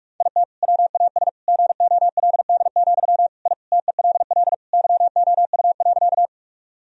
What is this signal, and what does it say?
Morse code, heard as a keyed tone, about 46 wpm, 700 Hz. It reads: IT WAS GOLD, I TELL YOU.
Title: Sentences from Top 1000 Words - Review Part 3 - 45wpm